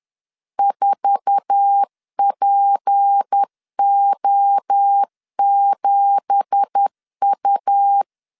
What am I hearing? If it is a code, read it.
Morse code, heard as a keyed tone, about 10 wpm, 800 Hz. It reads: 4PO7U
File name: LEZIONI / 9 2867 10wpm 2733